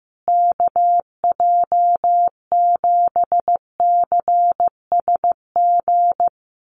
Morse code, heard as a keyed tone, about 15 wpm, 700 Hz. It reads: KJ7CSG
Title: USA Call Signs Repeated 2x 15wpm